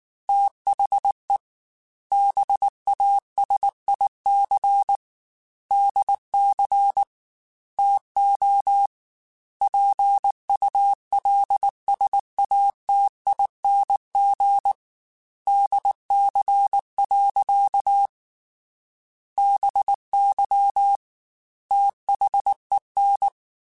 Morse code, heard as a keyed tone, about 19 wpm, 800 Hz. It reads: THE BASIC DC TO PULSATING DC. BY THEN
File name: Technical Subjects / Technical Subjects-mp3s / Sw Pwr Sup @18